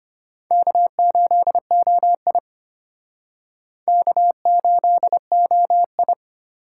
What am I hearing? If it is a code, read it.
Morse code, heard as a keyed tone, about 32 wpm, 700 Hz. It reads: K8OI K8OI